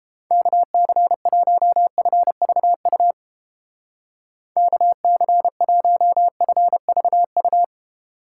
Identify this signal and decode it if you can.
Morse code, heard as a keyed tone, about 29 wpm, 700 Hz. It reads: KC1FVU KC1FVU